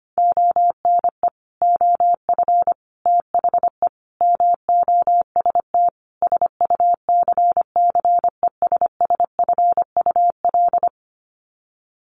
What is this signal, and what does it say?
Morse code, heard as a keyed tone, about 25 wpm, 700 Hz. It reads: ONE OF THE MOST SUCCESSFUL
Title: Sets of 5 Words Encore2 25wpm